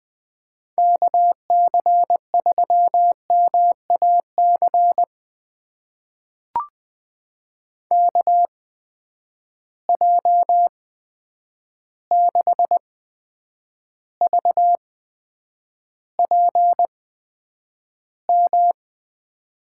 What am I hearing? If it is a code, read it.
Morse code, heard as a keyed tone, about 20 wpm, spaced as for 9 wpm, 700 Hz. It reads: KC3MAC KJ6VPM
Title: USA Call Signs - 6 Characters - 8x Character Spacing - 20wpm